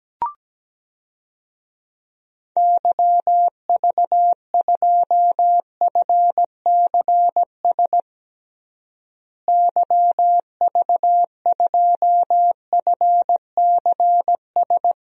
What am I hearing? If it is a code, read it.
Morse code, heard as a keyed tone, about 17 wpm, 700 Hz. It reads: YV2FCS YV2FCS
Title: DXCC-100 Call Signs - 6 Characters - Repeated 2x - 17wpm